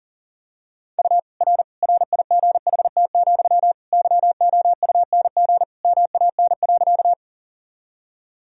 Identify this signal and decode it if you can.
Morse code, heard as a keyed tone, about 40 wpm, 700 Hz. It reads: U R RIGHT, YOUNG MAN.
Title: Sentences from Top 400 Words and Abbreviations 40wpm